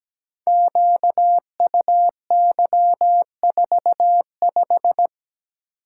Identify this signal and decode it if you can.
Morse code, heard as a keyed tone, about 17 wpm, 700 Hz. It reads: QUY45
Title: Code Groups - 5 Letter:Numbers - 17wpm